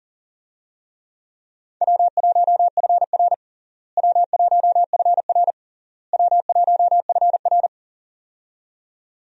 Morse code, heard as a keyed tone, about 37 wpm, 700 Hz. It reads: W1FR W1FR W1FR
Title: USA Call Signs - 4 Characters - Repeated 3x - 40wpm